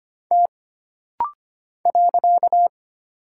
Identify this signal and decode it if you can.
Morse code, heard as a keyed tone, about 25 wpm, 700 Hz. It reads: T .
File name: Single Character (Letter:Number:Punctuation:Prosign) - Rapid-Fire - 25wpm